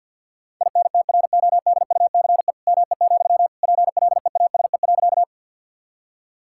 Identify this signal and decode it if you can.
Morse code, heard as a keyed tone, about 50 wpm, 700 Hz. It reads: INTRODUCE ME, PLEASE.
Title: Sentences from Top 1600 Words 50wpm